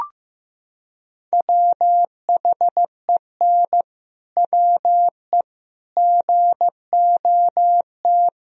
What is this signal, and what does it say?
Morse code, heard as a keyed tone, about 15 wpm, 700 Hz. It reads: WHEN WE GOT
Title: Sentences from Top 500 Words and Abbreviations 15wpm